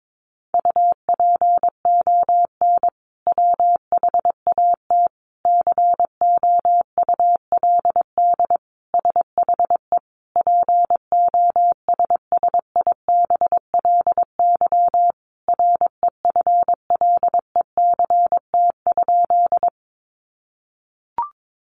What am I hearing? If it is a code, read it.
Morse code, heard as a keyed tone, about 22 wpm, 700 Hz. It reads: UPON WHAT COULD SHE POSSIBLY REFLECT?
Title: Sentences from Top 1500 Words 22wpm